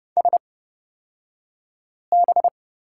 Morse code, heard as a keyed tone, about 30 wpm, spaced as for 5 wpm, 700 Hz. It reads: SB
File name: Lesson 38d - Letter q - Two Characters 30wpm